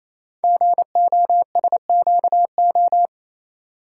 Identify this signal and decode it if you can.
Morse code, heard as a keyed tone, about 29 wpm, 700 Hz. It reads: GOSQO
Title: Code Groups - 5 Letters - 28wpm